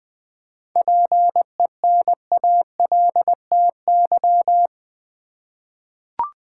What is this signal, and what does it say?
Morse code, heard as a keyed tone, about 20 wpm, 700 Hz. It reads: PENALTY
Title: Seven Letter Words - 20wpm